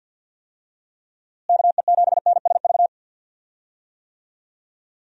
Morse code, heard as a keyed tone, about 50 wpm, 700 Hz. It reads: KE7NSV
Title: USA Call Signs - 6 Characters - Repeated 2x - 50wpm